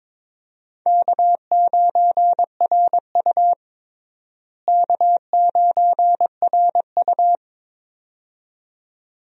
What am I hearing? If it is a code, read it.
Morse code, heard as a keyed tone, about 22 wpm, 700 Hz. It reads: K9RU K9RU
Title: USA Call Signs - 4 Characters - Repeated 2x - 22wpm